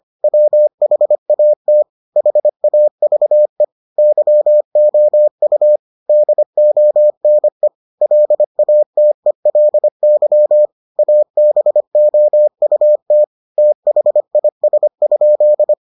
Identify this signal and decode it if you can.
Morse code, heard as a keyed tone, about 25 wpm, 600 Hz. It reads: WHAT HAVE YOU DONE LATELY ABOUT THIS?